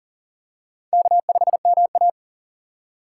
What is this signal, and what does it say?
Morse code, heard as a keyed tone, about 40 wpm, 700 Hz. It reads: K5MA
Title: K1USN SST QSO Practice - 40wpm